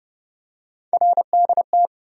Morse code, heard as a keyed tone, about 31 wpm, 700 Hz. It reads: RDT
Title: USA Call Sign Postfixes - 3 Letters 30wpm